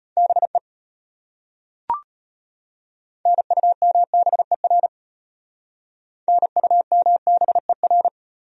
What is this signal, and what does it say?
Morse code, heard as a keyed tone, about 32 wpm, 700 Hz. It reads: BE NUMBER NUMBER